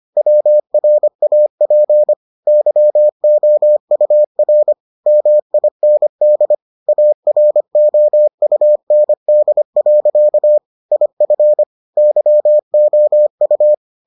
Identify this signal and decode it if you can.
Morse code, heard as a keyed tone, about 25 wpm, 600 Hz. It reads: WRAP YOUR MIND AROUND. IF YOU